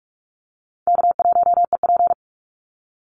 Morse code, heard as a keyed tone, about 44 wpm, 700 Hz. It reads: K1EP
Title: Easy Call Signs - 45wpm